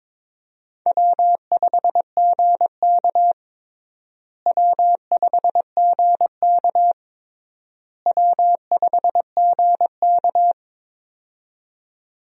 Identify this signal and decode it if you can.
Morse code, heard as a keyed tone, about 22 wpm, 700 Hz. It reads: W5GK W5GK W5GK
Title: USA Call Signs - 4 Characters - Repeated 3x - 22wpm